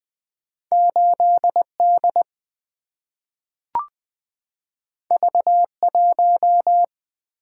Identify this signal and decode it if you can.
Morse code, heard as a keyed tone, about 20 wpm, 700 Hz. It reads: 8 D V 1
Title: Single-Letter Single-Number 20wpm